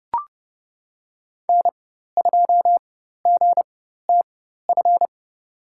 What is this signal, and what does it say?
Morse code, heard as a keyed tone, about 30 wpm, spaced as for 14 wpm, 700 Hz. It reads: N2GTF
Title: USA Call Signs - 5 Characters - 4x Character Spacing - 30wpm